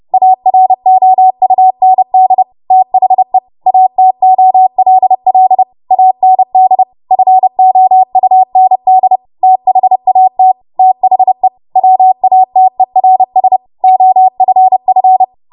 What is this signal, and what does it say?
Morse code, heard as a keyed tone, about 30 wpm, 750 Hz. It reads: AROUND THE ATOLL AND FOUND THAT THE WATERS OFF
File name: Morse / 30WPM